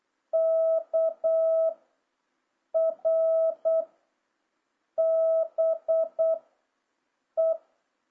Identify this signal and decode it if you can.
Morse code, heard as a keyed tone, about 8 wpm, 650 Hz. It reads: K R B E
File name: jh8cbh / 2026cw